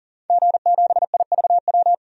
Morse code, heard as a keyed tone, about 37 wpm, 700 Hz. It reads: G7IVW
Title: DXCC-100 Call Signs - 5 Characters - Repeated 3x - 40wpm